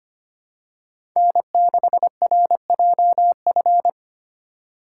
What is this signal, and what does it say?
Morse code, heard as a keyed tone, about 25 wpm, 700 Hz. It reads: N6RJF
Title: USA Call Signs - 5 Characters - 2x Character Spacing - 25wpm